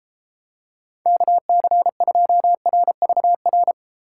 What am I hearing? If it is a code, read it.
Morse code, heard as a keyed tone, about 33 wpm, 700 Hz. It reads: KC2RVR